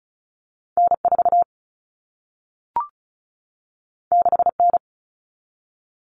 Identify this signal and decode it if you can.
Morse code, heard as a keyed tone, about 36 wpm, 700 Hz. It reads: N4 6N
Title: All Cut Numbers - 2-Digit 35wpm